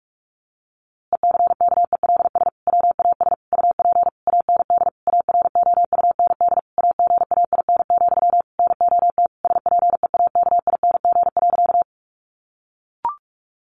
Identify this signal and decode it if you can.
Morse code, heard as a keyed tone, about 44 wpm, 700 Hz. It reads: ECKELS WAS UP AND AROUND AGAIN, NOT SPEAKING.
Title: A Sound Of Thunder 45wpm